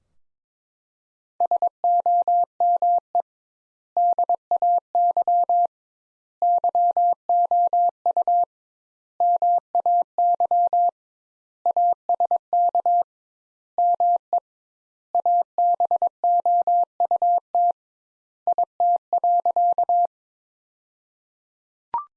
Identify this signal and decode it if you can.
Morse code, heard as a keyed tone, about 22 wpm, 700 Hz. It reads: SOME DAY YOU MAY ASK ME ABOUT IT.
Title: Sentences Top 200 Words - 22wpm